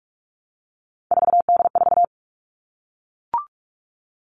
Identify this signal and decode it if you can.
Morse code, heard as a keyed tone, about 44 wpm, 700 Hz. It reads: 4D4
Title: All Cut Numbers - 3-Digit 45wpm